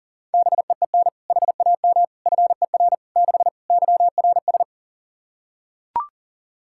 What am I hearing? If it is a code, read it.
Morse code, heard as a keyed tone, about 40 wpm, 700 Hz. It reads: BEEN HAM FER 6 YRS
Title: K1USN SST QSO Practice 40wpm